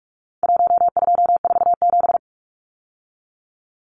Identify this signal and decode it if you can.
Morse code, heard as a keyed tone, about 44 wpm, 700 Hz. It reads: 1247
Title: Numbers - Contest Style - 4-Digit 45wpm